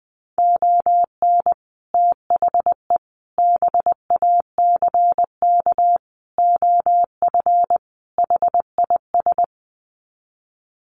Morse code, heard as a keyed tone, about 20 wpm, 700 Hz. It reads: ON THE BACK OF HIS
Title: Sets of 5 Words Encore 20wpm